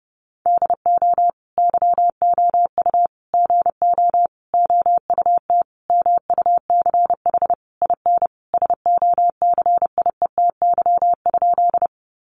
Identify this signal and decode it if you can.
Morse code, heard as a keyed tone, about 30 wpm, 700 Hz. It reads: DO YOU GO OUT MUCH IN SOCIETY?
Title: Sentences from Top 700 Words 30wpm